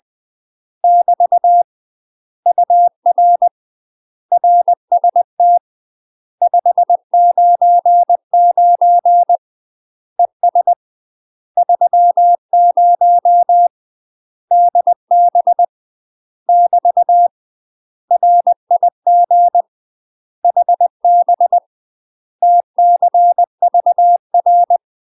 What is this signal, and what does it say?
Morse code, heard as a keyed tone, about 20 wpm, 700 Hz. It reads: <BT> UR RST 599 ES 30 DB <BT> RIG HB TCVR
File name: UK MASTER QSOs TRAINING/20WPM